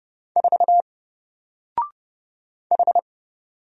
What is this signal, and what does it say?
Morse code, heard as a keyed tone, about 30 wpm, 700 Hz. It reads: 4 H
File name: Single-Letter-Number-30wpm